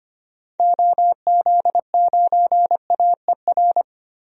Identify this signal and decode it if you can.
Morse code, heard as a keyed tone, about 25 wpm, 700 Hz. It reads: OZ9AER
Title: DXCC-100 Call Signs - 6 Characters - Repeated 2x - 25wpm